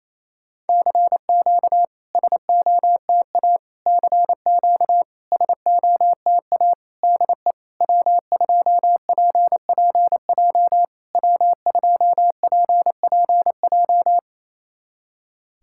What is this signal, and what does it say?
Morse code, heard as a keyed tone, about 28 wpm, 700 Hz. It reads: CQ SOTA CQ SOTA DE W2PPJ W2PPJ
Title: SOTA QSO Practice 28wpm